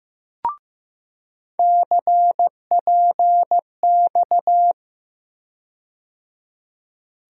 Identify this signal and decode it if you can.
Morse code, heard as a keyed tone, about 16 wpm, 700 Hz. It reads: CPX
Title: USA Call Sign Postfixes - 3 Letters 15wpm